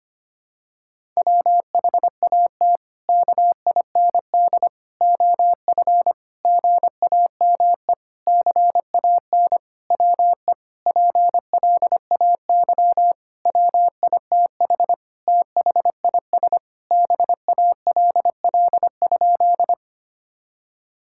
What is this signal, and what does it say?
Morse code, heard as a keyed tone, about 25 wpm, 700 Hz. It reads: WHAT KIND OF GAME CAN WE PLAY WITH THIS BALL?
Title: Sentences from Top 1000 Words - Review Part 6 - 25wpm